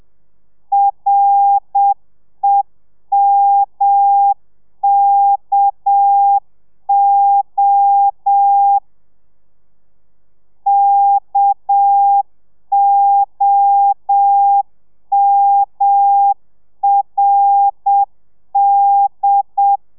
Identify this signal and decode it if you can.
Morse code, heard as a keyed tone, about 7 wpm, 800 Hz. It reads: REMKO KOMRD